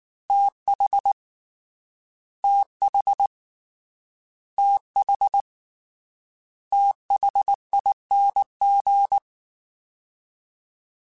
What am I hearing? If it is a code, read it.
Morse code, heard as a keyed tone, about 19 wpm, 800 Hz. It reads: TH TH TH THING
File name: TH / TH mp3s / TH Prefixes 3x2 Twice @18